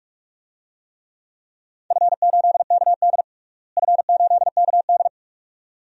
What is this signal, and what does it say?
Morse code, heard as a keyed tone, about 46 wpm, 700 Hz. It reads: F8KD F8KD
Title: DXCC-100 Call Signs - 4 Characters - Repeated 2x - 45wpm